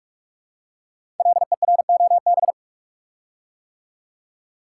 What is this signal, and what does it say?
Morse code, heard as a keyed tone, about 44 wpm, 700 Hz. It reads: LEROB
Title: Code Groups - 5 Letters - 45wpm